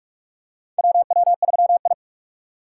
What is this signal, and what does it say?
Morse code, heard as a keyed tone, about 44 wpm, 700 Hz. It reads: WW3I
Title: USA Call Signs - 3x Character Spacing - 45wpm